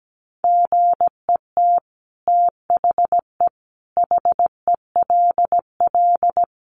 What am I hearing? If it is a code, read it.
Morse code, heard as a keyed tone, about 17 wpm, 700 Hz. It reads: GET THE HELL
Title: Sets of 5 Words Encore2 17wpm